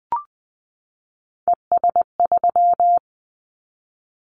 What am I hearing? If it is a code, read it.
Morse code, heard as a keyed tone, about 20 wpm, 700 Hz. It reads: ES3
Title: Callsign Prefixes for Top 50 DXCC - 20wpm